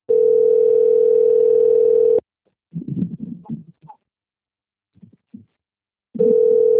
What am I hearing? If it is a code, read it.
Ringback tone (North America), 440 + 480 Hz.
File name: out- - - - - .59279